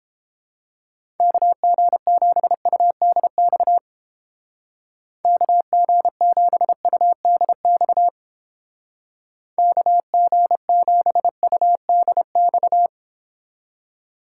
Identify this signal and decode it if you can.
Morse code, heard as a keyed tone, about 29 wpm, 700 Hz. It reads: KG7UDX KG7UDX KG7UDX